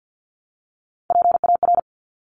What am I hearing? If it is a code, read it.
Morse code, heard as a keyed tone, about 50 wpm, 700 Hz. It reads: PAR
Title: Top Three Letter Combinations in Words 50wpm